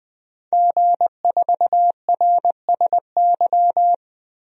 Morse code, heard as a keyed tone, about 21 wpm, 700 Hz. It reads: G4RSY
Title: DXCC-100 Call Signs - 5 Characters - 20wpm